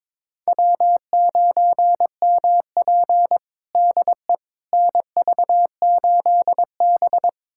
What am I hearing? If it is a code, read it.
Morse code, heard as a keyed tone, about 21 wpm, 700 Hz. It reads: W9MP DE NV8B
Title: USA Callsign DE Callsign - 4 Characters - 22wpm